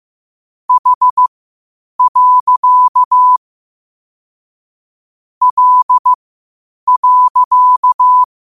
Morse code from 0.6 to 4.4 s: H.

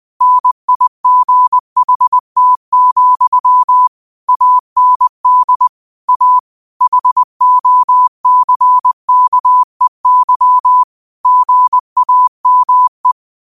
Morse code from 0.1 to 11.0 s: NIGHT, AND A HOCKEY

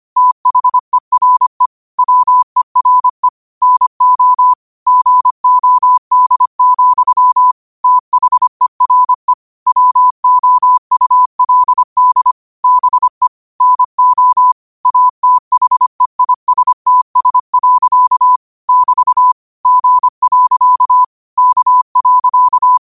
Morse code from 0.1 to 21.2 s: THERE WERE NO GOD, THERE WOULD BE NO ATHEISTS. <BT> G.

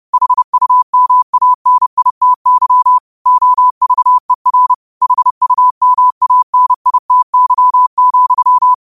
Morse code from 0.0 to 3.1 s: HUMANITY

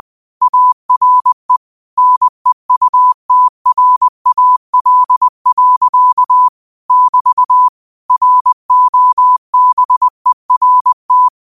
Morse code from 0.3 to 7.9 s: ARE NEUTRAL. <BT>

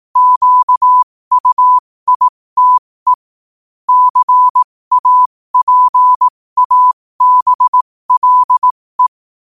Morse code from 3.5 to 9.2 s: CAPABLE

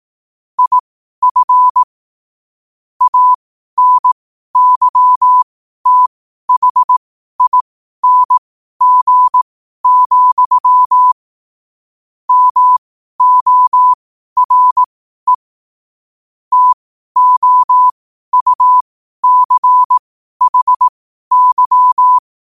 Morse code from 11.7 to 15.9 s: MORE

